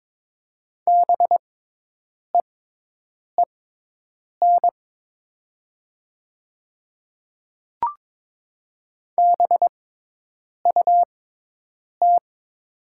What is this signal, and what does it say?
Morse code, heard as a keyed tone, about 22 wpm, spaced as for 8 wpm, 700 Hz. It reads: BEEN BUT
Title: 6x Character Spacing - Top 100 Words - 22wpm